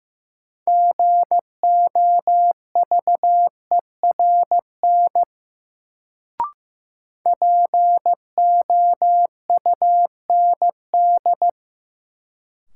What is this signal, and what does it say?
Morse code, heard as a keyed tone, about 15 wpm, 700 Hz. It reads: GOVERN POUND
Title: Sentences from Top 1000 Words - Review Part 2 - New Word Drill - 15wpm